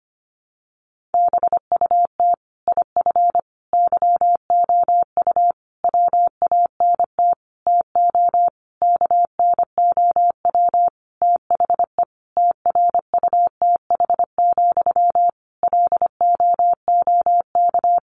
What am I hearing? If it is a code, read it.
Morse code, heard as a keyed tone, about 25 wpm, 700 Hz. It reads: BUT IF YOU WANT TO KNOW THE TRUTH, LOOK
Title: Sentences from Top 800 Words 25wpm